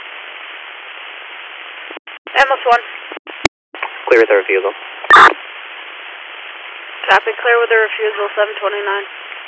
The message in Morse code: none

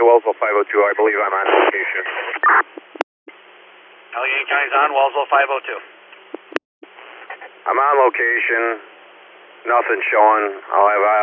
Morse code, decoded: none

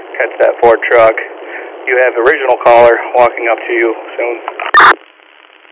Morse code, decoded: none

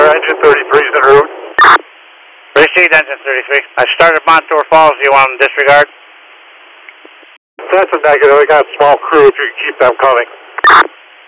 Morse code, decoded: none

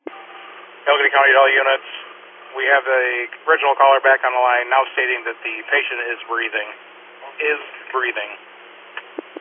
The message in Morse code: none

